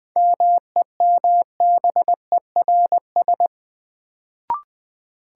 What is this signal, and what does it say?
Morse code, heard as a keyed tone, about 20 wpm, 700 Hz. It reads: MEMBERS
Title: Sets of 3 Words - New Word Drill - 20wpm